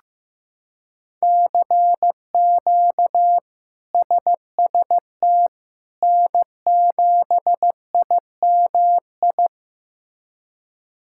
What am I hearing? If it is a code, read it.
Morse code, heard as a keyed tone, about 16 wpm, 700 Hz. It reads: CQ SST N7IMI
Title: K1USN SST QSO Practice - 15wpm